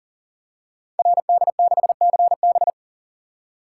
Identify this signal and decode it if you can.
Morse code, heard as a keyed tone, about 38 wpm, 700 Hz. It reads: RD6CB
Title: Code Groups - 5 Letter:Numbers - 40wpm